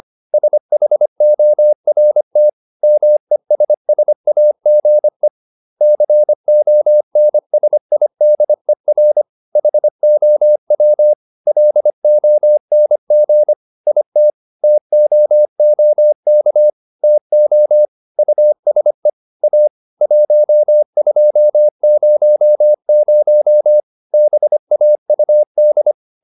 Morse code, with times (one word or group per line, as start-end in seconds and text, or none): 0.3-2.5 SHORT
2.8-5.3 MESSAGE
5.8-9.2 CONSIDER
9.6-11.1 HOW
11.5-13.5 LONG
13.9-14.3 IT
14.6-16.7 TOOK
17.0-17.9 TO
18.2-19.1 USE
19.4-19.7 A
20.0-23.8 1200
24.1-25.9 BAUD